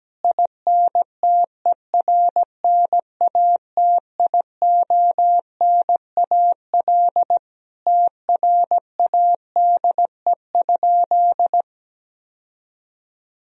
Morse code, heard as a keyed tone, about 17 wpm, 700 Hz. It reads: INTERNATIONAL TRADE?
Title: Sentences from Top 1000 Words - Review Part 4 - 17wpm